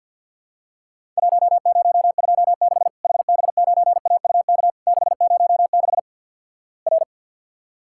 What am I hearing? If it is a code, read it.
Morse code, heard as a keyed tone, about 50 wpm, 700 Hz. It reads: 1026 HB9AUK 606 R
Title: Worked All Europe DX Contest - QTC Exchange - 50wpm